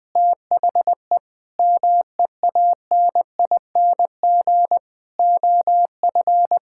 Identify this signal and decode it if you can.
Morse code, heard as a keyed tone, about 20 wpm, 700 Hz. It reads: THE MEANING OF
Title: Sentences from Top 1000 Words - Review Part 6 - 20wpm